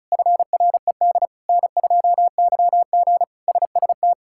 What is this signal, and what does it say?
Morse code, heard as a keyed tone, about 34 wpm, 700 Hz. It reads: FRED N2YG SST
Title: K1USN SST QSO Practice - 35wpm